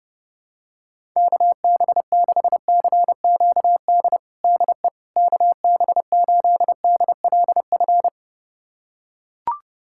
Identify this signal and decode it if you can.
Morse code, heard as a keyed tone, about 30 wpm, 700 Hz. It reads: KB6CQD DE KB8DLF
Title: USA Callsign DE Callsign 30wpm